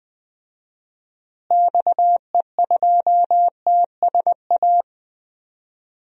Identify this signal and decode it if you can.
Morse code, heard as a keyed tone, about 20 wpm, 700 Hz. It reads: XE2TSA
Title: DXCC-100 Call Signs - 6 Characters - Repeated 2x - 20wpm